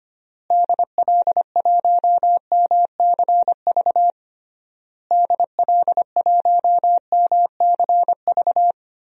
Morse code, heard as a keyed tone, about 25 wpm, 700 Hz. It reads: DL1MCV DL1MCV
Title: DXCC-100 Call Signs - 6 Characters - Repeated 3x - 25wpm